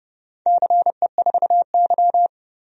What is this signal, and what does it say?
Morse code, heard as a keyed tone, about 30 wpm, 700 Hz. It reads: CE4Y
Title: DXCC-100 Call Signs - 4 Characters - 30wpm